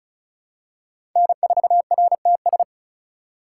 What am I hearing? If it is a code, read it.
Morse code, heard as a keyed tone, about 34 wpm, 700 Hz. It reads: N4RTS
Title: USA Call Signs - 3x Character Spacing - 35wpm